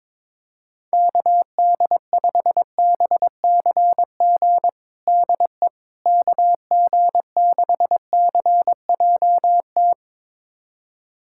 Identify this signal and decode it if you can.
Morse code, heard as a keyed tone, about 22 wpm, 700 Hz. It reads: KD5BCG DE KG6CJT
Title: USA Callsign DE Callsign 22wpm